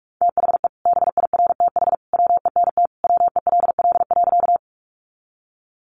Morse code, heard as a keyed tone, about 44 wpm, 700 Hz. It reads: THE BIRTH WENT WELL.
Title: Sentences from Top 800 Words 45wpm